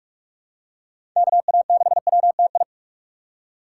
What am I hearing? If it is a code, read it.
Morse code, heard as a keyed tone, about 46 wpm, 700 Hz. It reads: KA6WTI
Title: USA Call Signs - 6 Characters - 4x Character Spacing - 45wpm